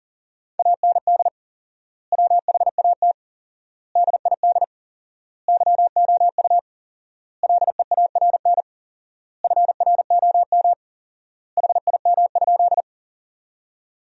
Morse code, heard as a keyed tone, about 37 wpm, 700 Hz. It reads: AND WHAT DID YOU LEARN FROM HIM?
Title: 4x Word-Spacing - Sentences from Top 300 Words - 40wpm